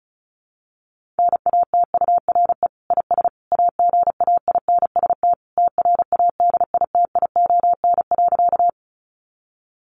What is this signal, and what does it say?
Morse code, heard as a keyed tone, about 34 wpm, 700 Hz. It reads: NATURE IS AGAINST TRADITION.
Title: Sentences from Top 1500 Words 35wpm